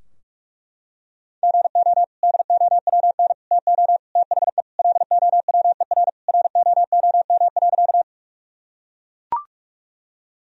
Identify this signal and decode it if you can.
Morse code, heard as a keyed tone, about 41 wpm, 700 Hz. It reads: GO DOWN TO THE LOWER ROOM.
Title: Sentences from Top 1000 Words - Review Part 4 - 45wpm